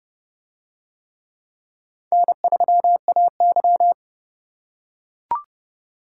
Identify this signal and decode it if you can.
Morse code, heard as a keyed tone, about 28 wpm, 700 Hz. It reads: N3AY